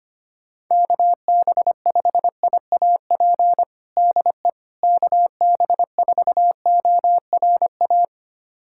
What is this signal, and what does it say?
Morse code, heard as a keyed tone, about 25 wpm, 700 Hz. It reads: KB5IAP DE KB4ORA
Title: USA Callsign DE Callsign - 6 Characters - 25wpm